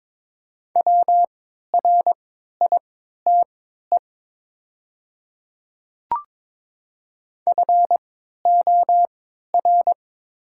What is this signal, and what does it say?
Morse code, heard as a keyed tone, about 22 wpm, spaced as for 12 wpm, 700 Hz. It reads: WRITE FOR
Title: 3x Character Spacing - Top 300 Words - 22wpm